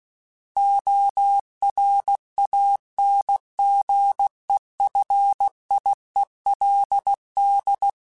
Morse code, heard as a keyed tone, about 16 wpm, 800 Hz. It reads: ORANGEFIELD
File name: fromSMs / FromSMs mp3s / Cities@15